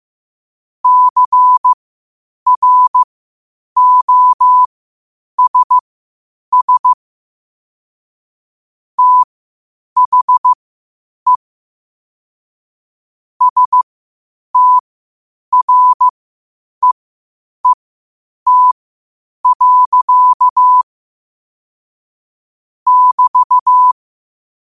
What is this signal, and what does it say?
Morse code, heard as a keyed tone, about 15 wpm, spaced as for 8 wpm, 1000 Hz. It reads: CROSS THE STREET. <BT>
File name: qotdfiles / 8wpm